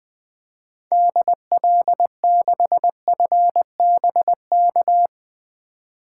Morse code, heard as a keyed tone, about 20 wpm, 700 Hz. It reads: DL6FBK
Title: 2023 SOTA Activator Call Signs - 20wpm